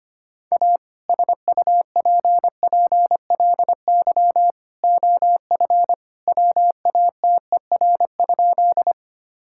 Morse code, heard as a keyed tone, about 25 wpm, 700 Hz. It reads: A SUPPLY OF WATER?
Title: Sentences from Top 1000 Words - Review Part 5 - 25wpm